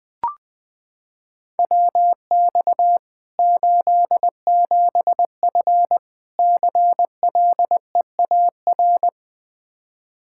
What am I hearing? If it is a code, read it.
Morse code, heard as a keyed tone, about 20 wpm, 700 Hz. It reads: WX 87F CLEAR
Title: Practice QSO - Exchange 2 - Info - 20wpm